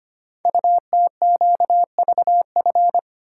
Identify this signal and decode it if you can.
Morse code, heard as a keyed tone, about 25 wpm, 700 Hz. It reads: UTQVF
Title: Code Groups - 5 Letters - 25wpm